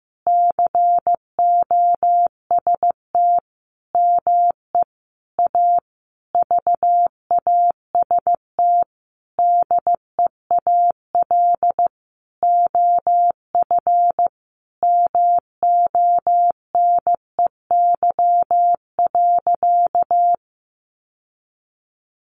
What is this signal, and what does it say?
Morse code, heard as a keyed tone, about 15 wpm, 700 Hz. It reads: COST ME A VAST DEAL OF MONEY.
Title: Sentences from Top 1300 Words 15wpm